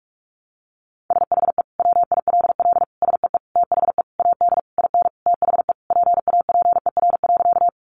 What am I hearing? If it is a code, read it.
Morse code, heard as a keyed tone, about 44 wpm, 700 Hz. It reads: SHE WILL SEE THE AD IN THE PAPER.
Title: Sentences from Top 1300 Words 45wpm